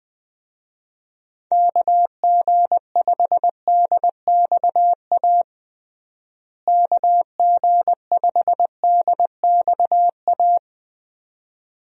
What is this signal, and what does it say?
Morse code, heard as a keyed tone, about 21 wpm, 700 Hz. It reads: KG5DXA KG5DXA
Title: USA Call Signs - 6 Characters - Repeated 2x - 20wpm